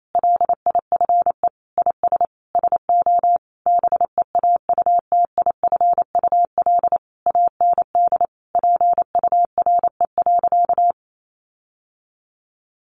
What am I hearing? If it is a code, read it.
Morse code, heard as a keyed tone, about 28 wpm, 700 Hz. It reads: LIFE IS SO BEAUTIFUL AND PURE.
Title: Sentences from Top 1500 Words 28wpm